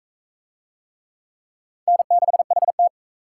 Morse code, heard as a keyed tone, about 40 wpm, 700 Hz. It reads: N6HT